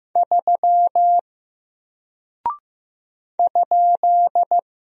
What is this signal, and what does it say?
Morse code, heard as a keyed tone, about 15 wpm, 700 Hz. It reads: 3 ?
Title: Single Character (Letter:Number:Punctuation:Prosign) Alt 15wpm